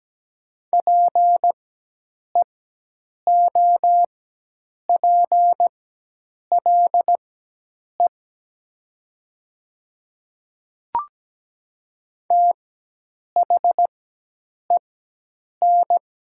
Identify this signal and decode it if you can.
Morse code, heard as a keyed tone, about 17 wpm, spaced as for 8 wpm, 700 Hz. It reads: PEOPLE THEN
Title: 4x Character Spacing - Top 200 Words - 17wpm